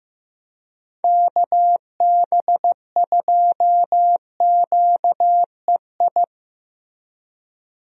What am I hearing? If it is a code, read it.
Morse code, heard as a keyed tone, about 15 wpm, 700 Hz. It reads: KB2QEI
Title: USA Call Signs - 6 Characters - Repeated 2x - 15wpm